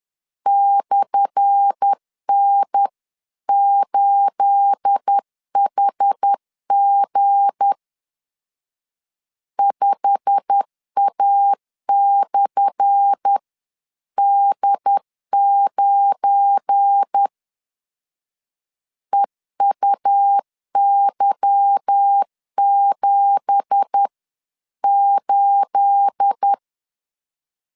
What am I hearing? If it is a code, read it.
Morse code, heard as a keyed tone, about 10 wpm, 800 Hz. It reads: /N8HG 5A/D9 EUY78
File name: LEZIONI / 10 SIGNS / 10A SLASH 10wpm 2808